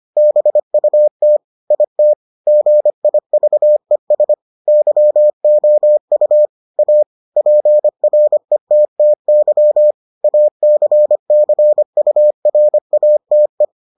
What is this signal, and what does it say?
Morse code, heard as a keyed tone, about 25 wpm, 600 Hz. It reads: BUT IT GIVES YOU A PRETTY ACCURATE